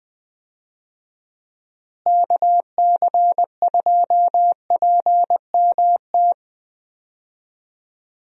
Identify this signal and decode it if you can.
Morse code, heard as a keyed tone, about 21 wpm, 700 Hz. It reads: KC2PMT
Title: USA Call Signs - 6 Characters - Repeated 2x - 20wpm